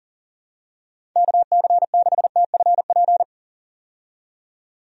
Morse code, heard as a keyed tone, about 40 wpm, 700 Hz. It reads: KC6TFP
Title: USA Call Signs - 6 Characters - 40wpm